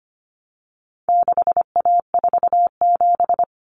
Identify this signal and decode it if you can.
Morse code, heard as a keyed tone, about 25 wpm, 700 Hz. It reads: 6A47
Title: Common Cut Numbers - 4-Digit 25wpm